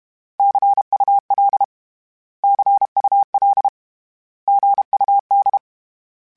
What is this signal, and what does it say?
Morse code, heard as a keyed tone, about 32 wpm, 800 Hz. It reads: CUL CUL GUD
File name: Abreviations / Abreviations mp3s / Abreviations@30